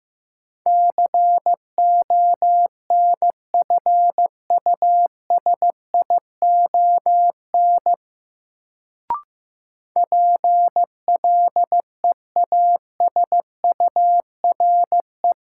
Morse code, heard as a keyed tone, about 15 wpm, 700 Hz. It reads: CONFUSION PLEASURE
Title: Sentences from Top 1600 Words - New Word Drill Alt - 15wpm